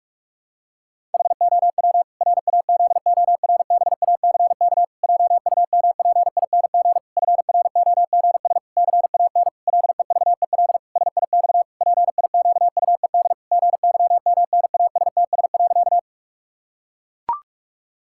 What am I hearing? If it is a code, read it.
Morse code, heard as a keyed tone, about 41 wpm, 700 Hz. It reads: HOW RAZORBACK JUMPING FROGS CAN LEVEL SIX PIQUED GYMNASTS.
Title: Pangram Challenge v2 - 45wpm